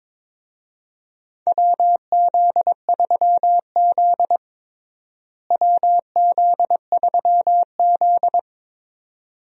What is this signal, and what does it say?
Morse code, heard as a keyed tone, about 22 wpm, 700 Hz. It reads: WZ3Z WZ3Z
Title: USA Call Signs - 4 Characters - Repeated 2x - 22wpm